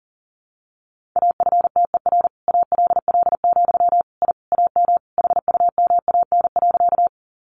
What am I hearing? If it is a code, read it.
Morse code, heard as a keyed tone, about 40 wpm, 700 Hz. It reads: AFTER ALL, I AM HUMAN.
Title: Sentences from Top 600 Words 40wpm